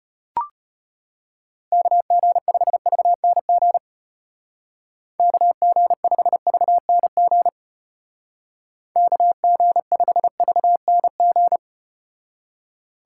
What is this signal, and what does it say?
Morse code, heard as a keyed tone, about 34 wpm, 700 Hz. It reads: KG5VNG KG5VNG KG5VNG